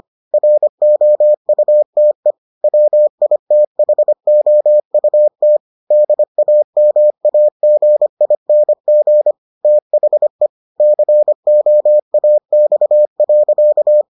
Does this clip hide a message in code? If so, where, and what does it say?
Yes: Morse code from 0.3 to 14.0 s, heard as a keyed tone, about 25 wpm, 600 Hz. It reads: ROUTE WITHOUT DAMAGING THE COAX.